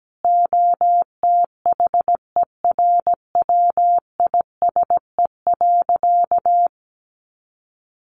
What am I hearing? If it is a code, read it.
Morse code, heard as a keyed tone, about 17 wpm, 700 Hz. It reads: OTHERWISE.
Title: Sentences from Top 1500 Words 17wpm